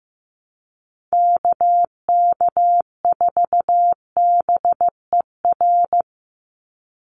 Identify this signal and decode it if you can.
Morse code, heard as a keyed tone, about 15 wpm, 700 Hz. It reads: KK4BER
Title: USA Call Signs 15wpm